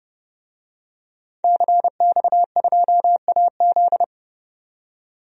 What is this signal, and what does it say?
Morse code, heard as a keyed tone, about 31 wpm, 700 Hz. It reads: CX2AZ
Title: DXCC-100 Call Signs - 5 Characters - 30wpm